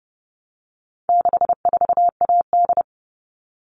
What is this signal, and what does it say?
Morse code, heard as a keyed tone, about 30 wpm, 700 Hz. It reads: 64AD
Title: All Cut Numbers - 4-Digit 30wpm